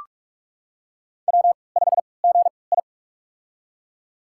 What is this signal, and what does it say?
Morse code, heard as a keyed tone, about 46 wpm, spaced as for 26 wpm, 690 Hz. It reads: W5GI
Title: USA Call Signs - 4 Characters - 3x Character Spacing - 45wpm